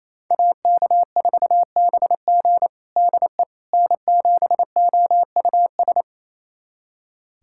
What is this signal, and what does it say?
Morse code, heard as a keyed tone, about 28 wpm, 700 Hz. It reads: AK4BG DE N7OUS
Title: USA Callsign DE Callsign - 5 Characters - 28wpm